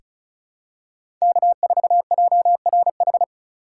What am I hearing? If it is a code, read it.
Morse code, heard as a keyed tone, about 36 wpm, 700 Hz. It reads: K4JRH
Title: USA Call Signs - 5 Characters - 3x Character Spacing - 35wpm